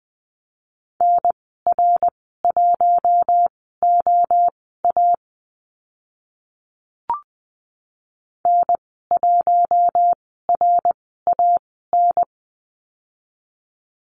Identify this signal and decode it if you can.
Morse code, heard as a keyed tone, about 20 wpm, spaced as for 14 wpm, 700 Hz. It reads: NR1OA N1RAN
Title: 069 Lesson 12g - Letter d - Call Signs 20wpm